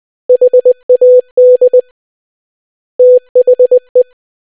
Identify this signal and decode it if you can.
Morse code, heard as a keyed tone, about 20 wpm, 500 Hz. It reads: HAD THE